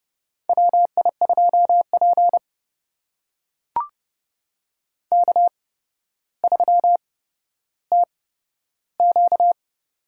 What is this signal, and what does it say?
Morse code, heard as a keyed tone, about 30 wpm, spaced as for 13 wpm, 700 Hz. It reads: WI2P K3TQ
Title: USA Call Signs - 4 Characters - 8x Character Spacing - 30wpm